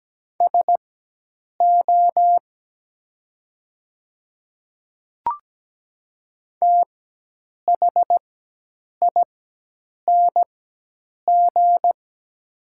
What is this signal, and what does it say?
Morse code, heard as a keyed tone, about 17 wpm, spaced as for 8 wpm, 700 Hz. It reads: SO THING
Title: 4x Character Spacing - Top 100 Words - 17wpm